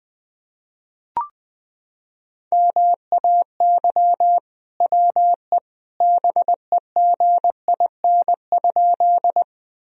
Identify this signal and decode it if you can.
Morse code, heard as a keyed tone, about 21 wpm, 700 Hz. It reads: MAY WE BEGIN?
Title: Sentences from Top 1000 Words - Review Part 1 - 20wpm